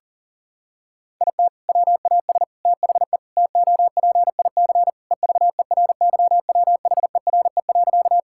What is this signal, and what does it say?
Morse code, heard as a keyed tone, about 40 wpm, 700 Hz. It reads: IT WAS THE TOPIC EVERYWHERE.
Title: Sentences from Top 1600 Words 40wpm